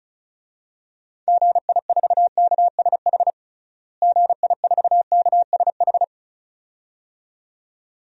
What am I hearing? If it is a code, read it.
Morse code, heard as a keyed tone, about 34 wpm, 700 Hz. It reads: GI4KSH GI4KSH
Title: DXCC-100 Call Signs - 6 Characters - Repeated 2x - 35wpm